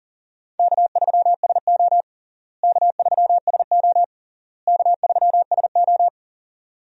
Morse code, heard as a keyed tone, about 40 wpm, 700 Hz. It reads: K3SO K3SO K3SO
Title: USA Call Signs - 4 Characters - Repeated 3x - 40wpm